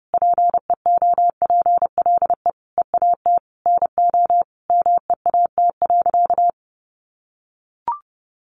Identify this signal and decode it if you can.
Morse code, heard as a keyed tone, about 30 wpm, 700 Hz. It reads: PEOPLE EAT NO MEAT.
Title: Sentences from Top 1300 Words 30wpm